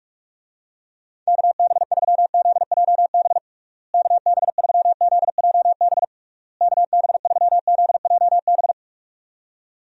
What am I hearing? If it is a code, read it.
Morse code, heard as a keyed tone, about 46 wpm, 700 Hz. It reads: KB3ZJB KB3ZJB KB3ZJB
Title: USA Call Signs - 6 Characters - Repeated 3x - 45wpm